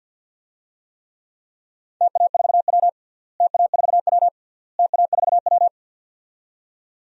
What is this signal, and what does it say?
Morse code, heard as a keyed tone, about 51 wpm, 700 Hz. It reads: TA4W TA4W TA4W
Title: DXCC-100 Call Signs - 4 Characters - Repeated 3x - 50wpm